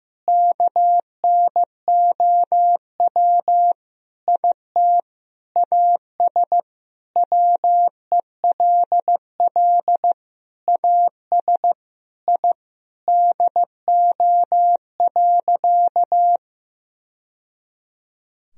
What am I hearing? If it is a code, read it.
Morse code, heard as a keyed tone, about 16 wpm, 700 Hz. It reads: KNOW IT AS WELL AS I DO.
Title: Sentences from Top 1000 Words - Review Part 3 - 15wpm